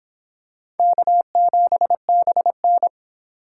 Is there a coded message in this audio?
Yes: Morse code from 0.8 to 2.9 s, heard as a keyed tone, about 26 wpm, 700 Hz. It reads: K7BN